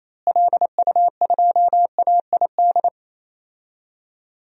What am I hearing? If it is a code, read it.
Morse code, heard as a keyed tone, about 28 wpm, 700 Hz. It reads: LU2AID
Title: DXCC-100 Call Signs - 6 Characters - Repeated 3x - 28wpm